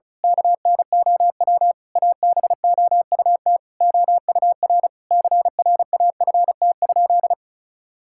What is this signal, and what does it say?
Morse code, heard as a keyed tone, about 34 wpm, 700 Hz. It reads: KNOW ABOUT OUR CRAFT?
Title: Sentences from Top 1000 Words - Review Part 6 - 35wpm